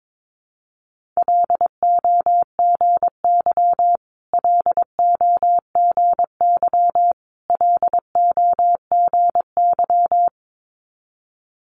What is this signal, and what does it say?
Morse code, heard as a keyed tone, about 22 wpm, 700 Hz. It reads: LOGY LOGY LOGY
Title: Multi-letter Phonemes Repeated 3x 22wpm